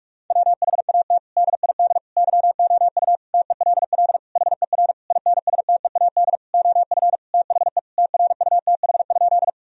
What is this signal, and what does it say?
Morse code, heard as a keyed tone, about 41 wpm, 700 Hz. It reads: WHAT DID YOU TELL HER INSTEAD OF THE TRUTH?
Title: Sentences from Top 1000 Words - Review Part 4 - 45wpm